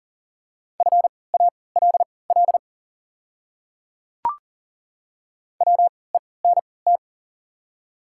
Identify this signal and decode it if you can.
Morse code, heard as a keyed tone, about 40 wpm, spaced as for 23 wpm, 690 Hz. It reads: FALL WENT
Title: 3x Character Spacing - Top 500 Words - 40wpm